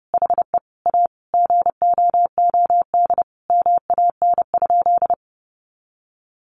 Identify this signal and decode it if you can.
Morse code, heard as a keyed tone, about 28 wpm, 700 Hz. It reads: HE A GOOD MAN?
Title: Sentences from Top 200 Words 30wpm